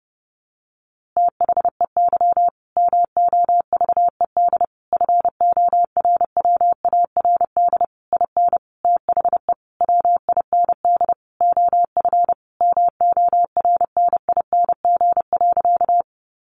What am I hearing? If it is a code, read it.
Morse code, heard as a keyed tone, about 30 wpm, 700 Hz. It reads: THEY MOVED FORWARD IN THE WIND OF MORNING.
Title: A Sound Of Thunder 30wpm